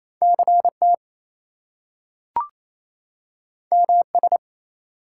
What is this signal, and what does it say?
Morse code, heard as a keyed tone, about 28 wpm, 700 Hz. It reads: CT MS
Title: US States and Canadian Provinces 28wpm